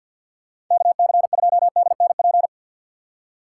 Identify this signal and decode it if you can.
Morse code, heard as a keyed tone, about 49 wpm, 700 Hz. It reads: KC2DNP
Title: USA Call Signs - 6 Characters - 8x Character Spacing - 50wpm